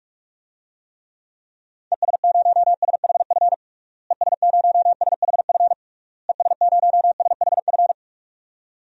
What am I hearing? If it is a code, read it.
Morse code, heard as a keyed tone, about 46 wpm, 700 Hz. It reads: ES0SHF ES0SHF ES0SHF
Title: DXCC-100 Call Signs - 6 Characters - Repeated 3x - 45wpm